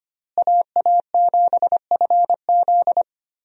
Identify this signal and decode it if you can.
Morse code, heard as a keyed tone, about 25 wpm, 700 Hz. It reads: AA7FZ
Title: USA Call Signs - 5 Characters - Repeated 2x - 25wpm